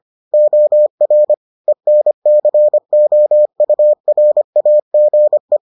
Morse code, heard as a keyed tone, about 25 wpm, 600 Hz. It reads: OR ENCOURAGE